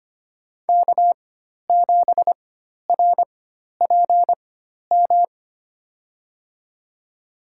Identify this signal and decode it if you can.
Morse code, heard as a keyed tone, about 25 wpm, spaced as for 12 wpm, 700 Hz. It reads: K7RPM
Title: USA Call Signs - 4x Character Spacing - 25wpm